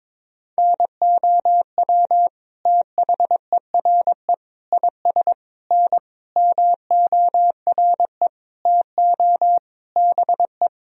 Morse code, heard as a keyed tone, about 21 wpm, 700 Hz. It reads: NOW THERE IS N MORE TO BE
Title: Sentences from Top 100 Words and Abbreviations 22wpm